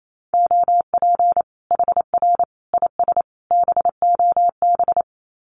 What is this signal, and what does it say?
Morse code, heard as a keyed tone, about 28 wpm, 700 Hz. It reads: OP HR IS BOB
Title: Common QSO Phrases 28wpm